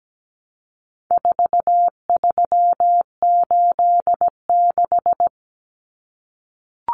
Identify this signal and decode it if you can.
Morse code, heard as a keyed tone, about 17 wpm, 700 Hz. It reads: 4386
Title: Common Cut Numbers - 4-Digit 17wpm